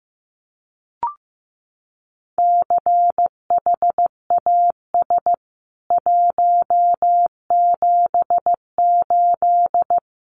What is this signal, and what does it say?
Morse code, heard as a keyed tone, about 15 wpm, 700 Hz. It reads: CHAS 178
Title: CWT Exchange 15wpm